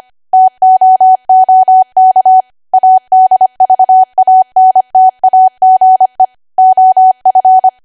Morse code, heard as a keyed tone, about 25 wpm, 750 Hz. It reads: TOOK ADVANTAGE OF
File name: cwa / W1AW-Level3 / 25WPM